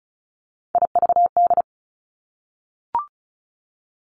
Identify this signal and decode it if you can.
Morse code, heard as a keyed tone, about 36 wpm, 700 Hz. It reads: IVD
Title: Three Challenging Letter-Number Combinations 35wpm